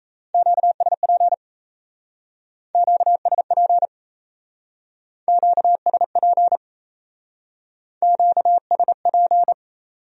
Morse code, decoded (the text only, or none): QSP QSP QSP QSP